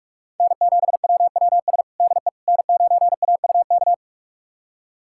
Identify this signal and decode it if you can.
Morse code, heard as a keyed tone, about 41 wpm, 700 Hz. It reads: N7WWS DE N9AUK
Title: USA Callsign DE Callsign - 5 Characters - 45wpm